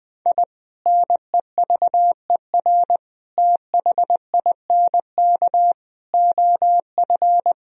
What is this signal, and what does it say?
Morse code, heard as a keyed tone, about 20 wpm, 700 Hz. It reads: I NEVER THINK OF
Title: Sentences from Top 1000 Words - Review Part 1 - 20wpm